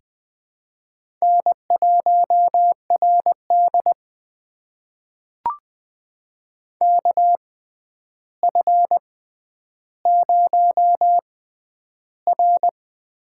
Morse code, heard as a keyed tone, about 20 wpm, spaced as for 10 wpm, 700 Hz. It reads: N1RD KF0R
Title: USA Call Signs - 4 Characters - 6x Character Spacing - 20wpm